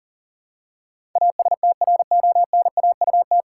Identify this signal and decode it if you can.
Morse code, heard as a keyed tone, about 40 wpm, 700 Hz. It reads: ASTRONAUT
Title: Pangram Challenge v2 - 40wpm